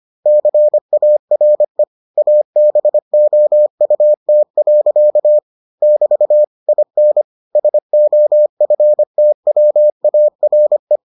Morse code, heard as a keyed tone, about 25 wpm, 600 Hz. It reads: CARE ABOUT. <BT> IN SOFTWARE